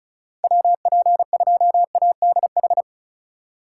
Morse code, heard as a keyed tone, about 36 wpm, 700 Hz. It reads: WP2ADH
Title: USA Call Signs - 6 Characters - 2x Character Spacing - 35wpm